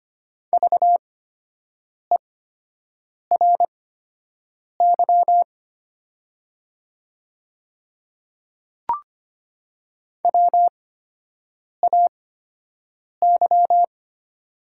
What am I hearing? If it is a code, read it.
Morse code, heard as a keyed tone, about 25 wpm, spaced as for 7 wpm, 700 Hz. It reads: VERY WAY
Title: 8x Character Spacing - Top 200 Words - 25wpm